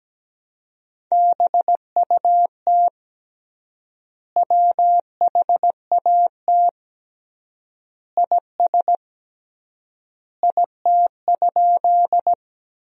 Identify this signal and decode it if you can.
Morse code, heard as a keyed tone, about 17 wpm, 700 Hz. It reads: BUT WHAT IS IT?
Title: 3x Word-Spacing - Sentences from Top 100 Words - 17wpm